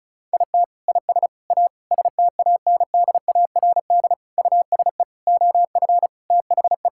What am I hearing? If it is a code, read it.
Morse code, heard as a keyed tone, about 34 wpm, 700 Hz. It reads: IT IS A STANDARD USE OF THE
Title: Sentences from Top 1000 Words - Review Part 6 - 35wpm